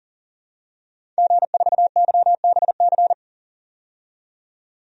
Morse code, heard as a keyed tone, about 40 wpm, 700 Hz. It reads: G4YBC
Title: DXCC-100 Call Signs - 5 Characters - 40wpm